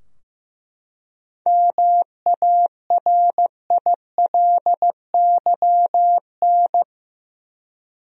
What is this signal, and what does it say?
Morse code, heard as a keyed tone, about 15 wpm, 700 Hz. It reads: MARILYN
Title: USA Common Names - 15wpm